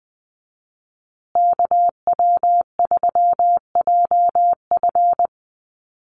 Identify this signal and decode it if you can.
Morse code, heard as a keyed tone, about 20 wpm, 700 Hz. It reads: KW3JF
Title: USA Call Signs Repeated 3x 20wpm